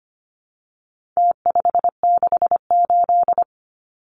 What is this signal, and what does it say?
Morse code, heard as a keyed tone, about 25 wpm, 700 Hz. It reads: T568
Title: Numbers - Contest Style - 4-Digit 25wpm